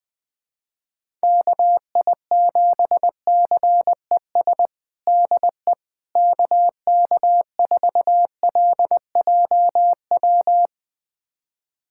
Morse code, heard as a keyed tone, about 21 wpm, 700 Hz. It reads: KI7CES DE KK4LJW
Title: USA Callsign DE Callsign 20wpm